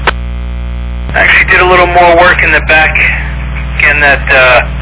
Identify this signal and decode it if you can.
Emergency Alert System (SAME) tones, heard as harsh buzzing bursts, with speech. It is heard from 1.1 to 4.2 s.